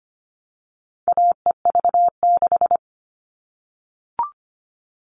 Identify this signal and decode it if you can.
Morse code, heard as a keyed tone, about 25 wpm, 700 Hz. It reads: AEV6
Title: All Cut Numbers - 4-Digit 25wpm